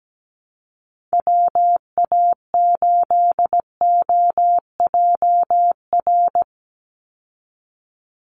Morse code, heard as a keyed tone, about 17 wpm, 700 Hz. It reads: WA8OJR
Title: Call Signs Repeated 2x - 17wpm